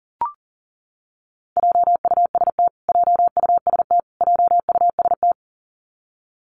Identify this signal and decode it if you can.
Morse code, heard as a keyed tone, about 40 wpm, 700 Hz. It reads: JUST JUST JUST
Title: Top 500 Words Repeated 40wpm